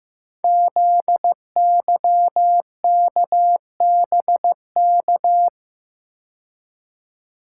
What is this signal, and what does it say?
Morse code, heard as a keyed tone, about 15 wpm, 700 Hz. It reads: ZYKBK
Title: Code Groups - 5 Letters - 15wpm